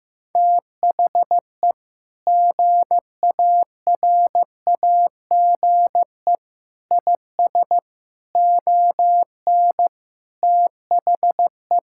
Morse code, heard as a keyed tone, about 15 wpm, 700 Hz. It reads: THE GARAGE IS ON THE
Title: Sentences from Top 1600 Words 15wpm